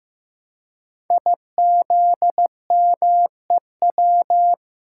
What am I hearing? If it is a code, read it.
Morse code, heard as a keyed tone, about 15 wpm, 700 Hz. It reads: IZMEW
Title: Code Groups - 5 Letters - 15wpm